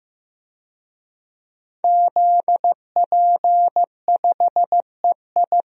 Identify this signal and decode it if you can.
Morse code, heard as a keyed tone, about 15 wpm, 700 Hz. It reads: ZP5EI
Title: DXCC-100 Call Signs - 5 Characters - Repeated 2x - 15wpm